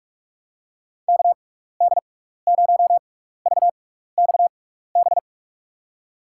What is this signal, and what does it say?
Morse code, heard as a keyed tone, about 46 wpm, spaced as for 15 wpm, 700 Hz. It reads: KD0VXB
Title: USA Call Signs - 6 Characters - 6x Character Spacing - 45wpm